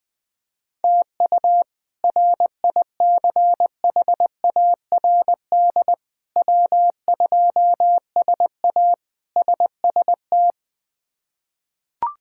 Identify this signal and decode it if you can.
Morse code, heard as a keyed tone, about 20 wpm, 700 Hz. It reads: TU RICHARD W2SA SST
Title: K1USN SST QSO Practice - 20wpm